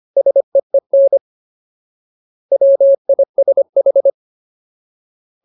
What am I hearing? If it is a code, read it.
Morse code, heard as a keyed tone, about 25 wpm, 550 Hz. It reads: SEEN WISH
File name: Ultimate Word List / 25WPM Ultimate